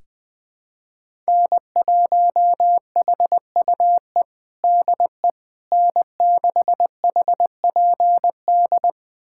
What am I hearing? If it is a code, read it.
Morse code, heard as a keyed tone, about 20 wpm, 700 Hz. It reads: N1HUE DE N6HPD
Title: USA Callsign DE Callsign 20wpm